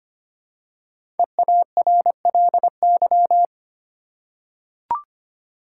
Morse code, heard as a keyed tone, about 25 wpm, 700 Hz. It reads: EARLY
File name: 5 Letter Words - First 3 Letters Given - 25wpm